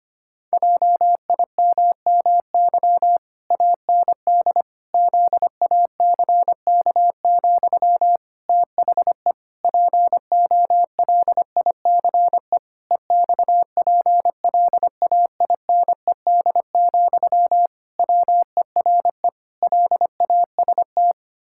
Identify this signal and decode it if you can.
Morse code, heard as a keyed tone, about 25 wpm, 700 Hz. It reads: JIMMY AND ZACK, THE POLICE EXPLAINED, WERE LAST
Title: Pangram Challenge v2 - 25wpm